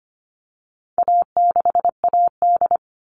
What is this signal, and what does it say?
Morse code, heard as a keyed tone, about 25 wpm, 700 Hz. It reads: A6AD
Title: All Cut Numbers - 4-Digit 25wpm